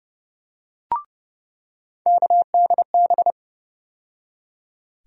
Morse code, heard as a keyed tone, about 30 wpm, 700 Hz. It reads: KDB
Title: USA Call Sign Postfixes - 3 Letters 30wpm